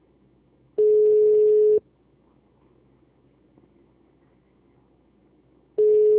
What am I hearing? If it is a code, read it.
Ringback tone (Europe, 425 Hz).
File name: out- -2006- - - .178